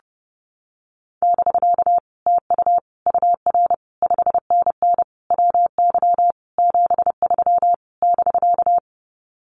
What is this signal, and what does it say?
Morse code, heard as a keyed tone, about 30 wpm, 700 Hz. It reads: <BK> TU UR 5NN WY 73 <BK>
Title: Common QSO Phrases Encore 30wpm